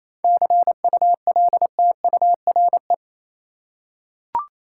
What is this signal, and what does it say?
Morse code, heard as a keyed tone, about 28 wpm, 700 Hz. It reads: CULTURE
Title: Seven Letter Words - 28wpm